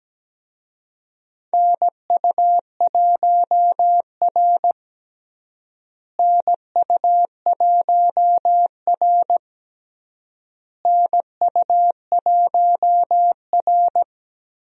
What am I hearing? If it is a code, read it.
Morse code, heard as a keyed tone, about 17 wpm, 700 Hz. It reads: NU1R NU1R NU1R
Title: USA Call Signs - 4 Characters - Repeated 3x - 17wpm